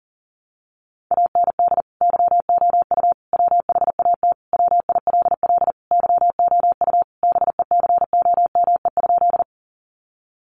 Morse code, heard as a keyed tone, about 40 wpm, 700 Hz. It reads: AND YOU WHAT WILL YOU BECOME?
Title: Sentences from Top 600 Words 40wpm